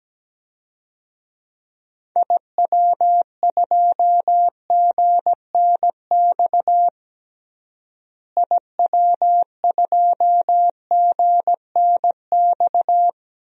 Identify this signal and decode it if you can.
Morse code, heard as a keyed tone, about 17 wpm, 700 Hz. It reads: IW2GNX IW2GNX
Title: DXCC-100 Call Signs - 6 Characters - Repeated 3x - 17wpm